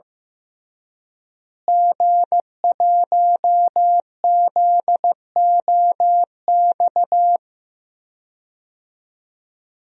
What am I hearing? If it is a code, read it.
Morse code, heard as a keyed tone, about 15 wpm, 700 Hz. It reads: G1ZOX
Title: DXCC-100 Call Signs - 5 Characters - Repeated 3x - 15wpm